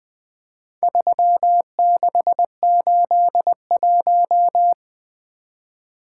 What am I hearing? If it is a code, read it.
Morse code, heard as a keyed tone, about 20 wpm, 700 Hz. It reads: 3681
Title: Four Letter-Number Combinations - 20wpm